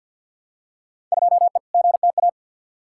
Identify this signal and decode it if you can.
Morse code, heard as a keyed tone, about 53 wpm, 700 Hz. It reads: 2E GTA
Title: Field Day Exchanges Encore - 50wpm